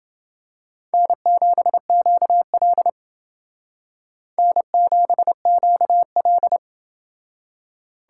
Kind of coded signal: Morse code

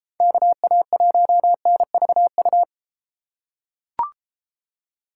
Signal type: Morse code

